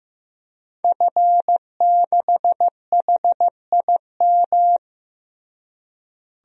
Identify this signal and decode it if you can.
Morse code, heard as a keyed tone, about 15 wpm, 700 Hz. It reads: F6HIM
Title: DXCC-100 Call Signs - 5 Characters - Repeated 2x - 15wpm